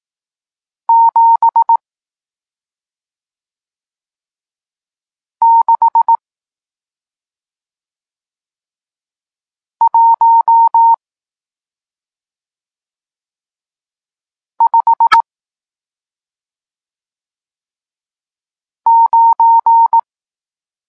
Morse code, heard as a keyed tone, about 18 wpm, 920 Hz. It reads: 7 6 1 5 9
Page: Tekst 40 – Nauka znaków przestankowych